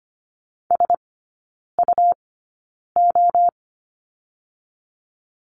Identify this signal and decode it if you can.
Morse code, heard as a keyed tone, about 25 wpm, spaced as for 9 wpm, 700 Hz. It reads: SUO
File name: Lesson 17e - Letter u - Three Characters 25wpm